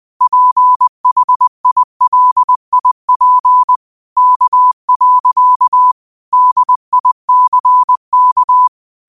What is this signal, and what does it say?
Morse code, heard as a keyed tone, about 20 wpm, 1000 Hz. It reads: PHILIP K. DICK
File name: qotdfiles / 20wpm